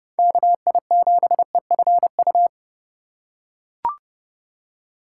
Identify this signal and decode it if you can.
Morse code, heard as a keyed tone, about 30 wpm, 700 Hz. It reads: KI7EFU
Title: USA Call Signs - 3x Character Spacing - 30wpm